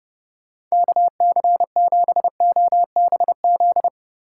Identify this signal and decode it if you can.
Morse code, heard as a keyed tone, about 36 wpm, 700 Hz. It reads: KC7OBZ
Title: Speed Racing - USA 2x3 Call Signs 30wpm